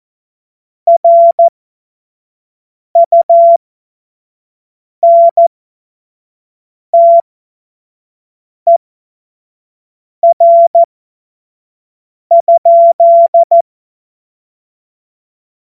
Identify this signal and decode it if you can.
Morse code, heard as a keyed tone, about 14 wpm, spaced as for 5 wpm, 680 Hz. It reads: RUNTER?
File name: Morsekurs / Witze / Witz 03. Januar 2023 25 F70